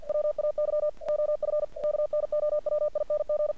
Morse code, heard as a keyed tone, about 48 wpm, 600 Hz. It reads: WAY OF KNOWING
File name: archived / CRSnet